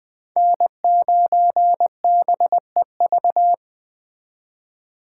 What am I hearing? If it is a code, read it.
Morse code, heard as a keyed tone, about 23 wpm, 700 Hz. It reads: N9BEV
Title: Speed Racing - USA 1x3 Call Signs 20wpm